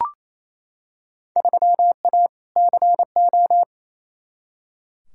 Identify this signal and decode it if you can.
Morse code, heard as a keyed tone, about 28 wpm, 700 Hz. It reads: 3A CO
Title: Field Day Exchanges Encore - 28wpm